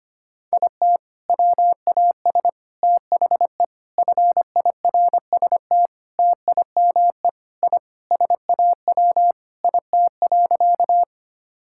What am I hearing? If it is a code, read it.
Morse code, heard as a keyed tone, about 25 wpm, 700 Hz. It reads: IT WAS THE FIRST TIME I SAW IT.
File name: practice / 25wpm / Sentences with 3-Word Phrases - 25wpm